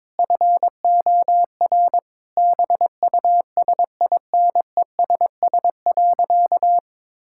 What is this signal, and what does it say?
Morse code, heard as a keyed tone, about 22 wpm, 700 Hz. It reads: FOR BUSINESS.
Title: Sentences from Top 1000 Words - Review Part 5 - 22wpm